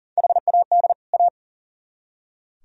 Morse code, heard as a keyed tone, about 51 wpm, 700 Hz. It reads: HAD A